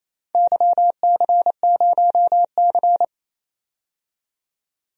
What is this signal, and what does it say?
Morse code, heard as a keyed tone, about 28 wpm, 700 Hz. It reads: YC0C
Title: DXCC-100 Call Signs - 4 Characters - 28wpm